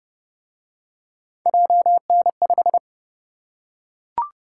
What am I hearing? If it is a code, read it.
Morse code, heard as a keyed tone, about 26 wpm, 700 Hz. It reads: JN5